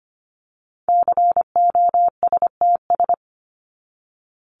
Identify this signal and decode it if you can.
Morse code, heard as a keyed tone, about 25 wpm, 700 Hz. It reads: COSTS
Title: ES:S Suffix - Top 100 Words with Base Words 25wpm